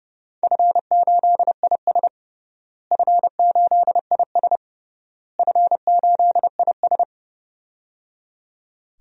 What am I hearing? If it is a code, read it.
Morse code, heard as a keyed tone, about 30 wpm, 700 Hz. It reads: F8IS F8IS F8IS
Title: DXCC-100 Call Signs - 4 Characters - Repeated 3x - 30wpm